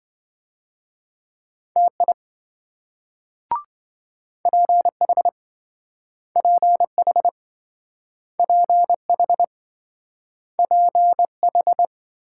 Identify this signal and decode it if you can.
Morse code, heard as a keyed tone, about 26 wpm, 700 Hz. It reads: TI PH PH PH PH